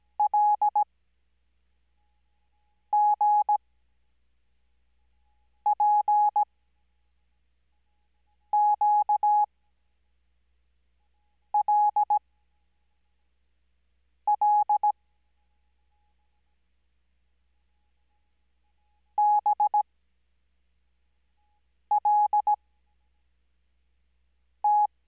Morse code, heard as a keyed tone, about 17 wpm, spaced as for 4 wpm, 840 Hz. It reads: LGPQLL BLT
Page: Farnsworth Spacing: 15WPM Character speed Spaced at 5WPM.